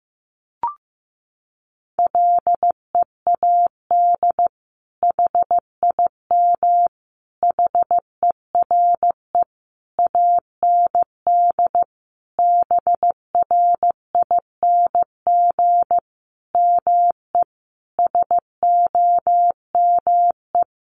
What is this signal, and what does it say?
Morse code, heard as a keyed tone, about 15 wpm, 700 Hz. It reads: LEAD HIM HERE AND BRING ME SOME
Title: Sentences from Top 1200 Words 15wpm